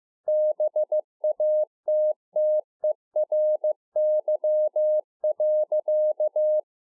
Morse code, heard as a keyed tone, about 15 wpm, 610 Hz. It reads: BATTERY.
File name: 2011-04-mp3 / A SMALL TRANCEIVER 2 15WPM